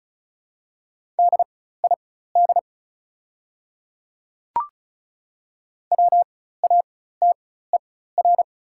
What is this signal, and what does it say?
Morse code, heard as a keyed tone, about 34 wpm, spaced as for 16 wpm, 700 Hz. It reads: DID WATER
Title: 4x Character Spacing - Top 100 Words - 35wpm